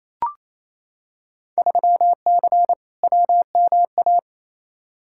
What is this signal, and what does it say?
Morse code, heard as a keyed tone, about 28 wpm, 700 Hz. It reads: 3C WMA
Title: Field Day Exchanges Encore - 28wpm